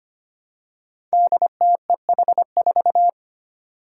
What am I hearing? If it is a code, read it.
Morse code, heard as a keyed tone, about 25 wpm, 700 Hz. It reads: DTEH4
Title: Five Challenging Letter-Number Combinations 25wpm